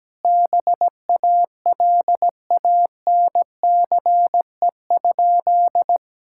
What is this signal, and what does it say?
Morse code, heard as a keyed tone, about 17 wpm, 700 Hz. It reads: BALANCE?
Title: Sentences from Top 1000 Words - Review Part 6 - 17wpm